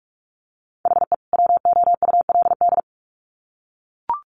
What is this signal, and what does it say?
Morse code, heard as a keyed tone, about 44 wpm, 700 Hz. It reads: HE WOULD
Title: Sets of 2 Words 45wpm